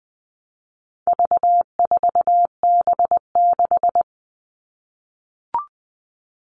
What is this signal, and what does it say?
Morse code, heard as a keyed tone, about 20 wpm, 700 Hz. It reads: V4B6
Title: All Cut Numbers - 4-Digit 20wpm